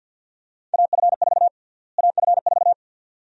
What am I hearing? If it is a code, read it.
Morse code, heard as a keyed tone, about 52 wpm, 700 Hz. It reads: AF4 AF4
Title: USA Call Sign Prefixes - Repeated 2x - 50wpm